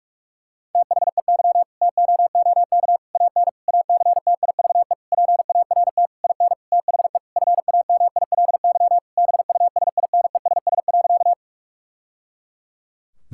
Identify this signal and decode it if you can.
Morse code, heard as a keyed tone, about 41 wpm, 700 Hz. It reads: THEY TOOK AN ACTIVE PART IN THE FAMILY BUSINESS.
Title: Sentences from Top 1000 Words - Review Part 6 - 45wpm